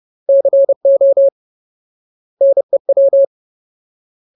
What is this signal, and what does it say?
Morse code, heard as a keyed tone, about 30 wpm, 550 Hz. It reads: CO NEW
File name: Ultimate Word List / 30WPM Ultimate